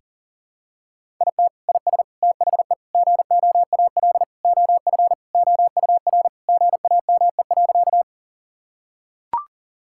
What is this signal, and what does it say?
Morse code, heard as a keyed tone, about 37 wpm, 690 Hz. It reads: IT IS THE GOAL OF OUR GAME.
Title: Sentences from Top 1000 Words - Review Part 5 - 40wpm